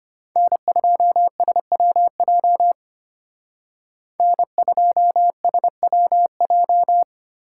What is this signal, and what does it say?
Morse code, heard as a keyed tone, about 31 wpm, 700 Hz. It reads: N2SWJ N2SWJ